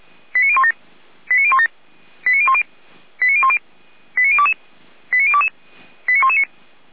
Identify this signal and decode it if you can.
Five-tone selective call (ZVEI), heard as quick runs of five tones, tones of about 70 ms. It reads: 89017 89017 89010 89010 89022 89022 89119